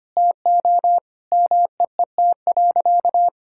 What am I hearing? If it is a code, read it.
Morse code, heard as a keyed tone, about 25 wpm, 700 Hz. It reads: TO MEET.
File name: practice / 25wpm / Sentences with 3-Word Phrases - 25wpm